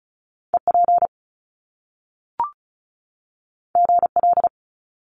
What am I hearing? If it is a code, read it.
Morse code, heard as a keyed tone, about 36 wpm, 700 Hz. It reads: EP GL
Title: Top Two Letter Combinations in Words 35wpm